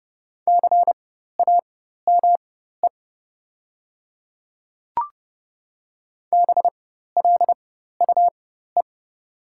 Morse code, heard as a keyed tone, about 29 wpm, spaced as for 14 wpm, 690 Hz. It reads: CAME BLUE
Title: 4x Character Spacing - Top 500 Words - 30wpm